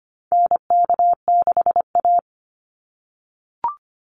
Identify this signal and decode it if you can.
Morse code, heard as a keyed tone, about 25 wpm, 700 Hz. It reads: NK6A
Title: NA SOTA Chaser Callsigns 25wpm